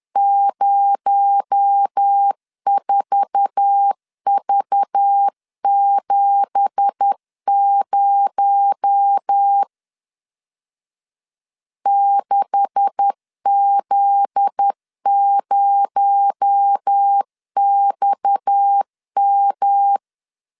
Morse code, heard as a keyed tone, about 10 wpm, 800 Hz. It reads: 04V70 6Z0XM